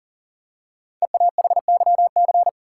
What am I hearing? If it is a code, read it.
Morse code, heard as a keyed tone, about 40 wpm, 700 Hz. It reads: EAHYC
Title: Code Groups - 5 Letters - 40wpm